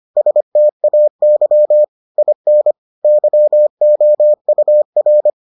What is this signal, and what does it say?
Morse code, heard as a keyed tone, about 25 wpm, 600 Hz. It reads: STAY IN YOUR